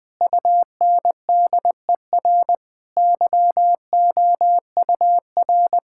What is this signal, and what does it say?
Morse code, heard as a keyed tone, about 21 wpm, 700 Hz. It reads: UNDER YOUR
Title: Sentences from Top 1600 Words 20wpm